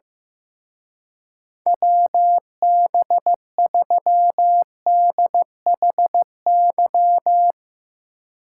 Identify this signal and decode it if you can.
Morse code, heard as a keyed tone, about 15 wpm, 700 Hz. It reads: WB3DHY
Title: Speed Racing - USA Call Signs 15wpm